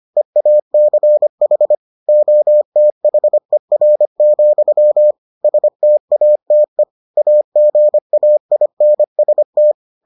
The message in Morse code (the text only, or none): EACH OTHER, STATE AGAINST